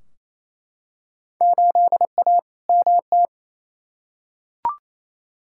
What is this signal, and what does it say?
Morse code, heard as a keyed tone, about 28 wpm, 700 Hz. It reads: 8A MT
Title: Field Day Exchanges Encore - 28wpm